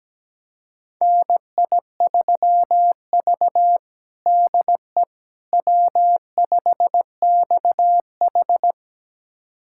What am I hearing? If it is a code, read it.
Morse code, heard as a keyed tone, about 17 wpm, 700 Hz. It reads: NI3V DE W5XH
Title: USA Callsign DE Callsign - 4 Characters - 17wpm